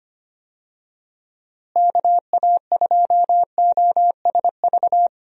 Morse code, heard as a keyed tone, about 25 wpm, 700 Hz. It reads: KA2OSV
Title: USA Call Signs - 6 Characters - Repeated 2x - 25wpm